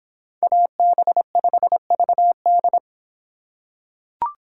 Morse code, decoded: AB5VD

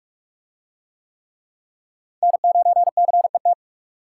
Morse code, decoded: N9CET

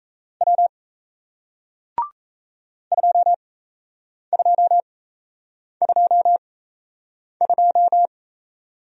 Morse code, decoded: W 2222

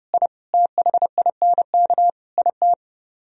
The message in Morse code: I THINK IT

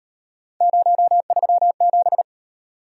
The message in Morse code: 037

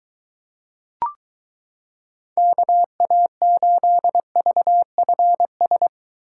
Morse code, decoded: KA8VFS

